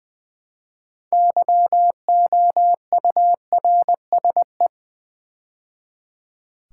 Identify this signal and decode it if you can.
Morse code, heard as a keyed tone, about 20 wpm, 690 Hz. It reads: YOURSE
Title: Build A Word - 3 Letters At A Time - 8-Letter Words - 20wpm